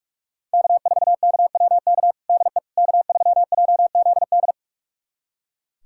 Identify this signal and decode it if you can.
Morse code, heard as a keyed tone, about 42 wpm, 700 Hz. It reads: K4KWK DE K3JZD
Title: USA Callsign DE Callsign - 5 Characters - 45wpm